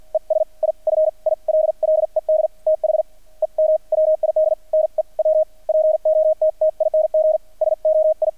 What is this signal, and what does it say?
Morse code, heard as a keyed tone, about 50 wpm, spaced as for 40 wpm, 630 Hz. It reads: ES I FILLED THEM WID NEW POTTING SOI
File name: archived / CRSnet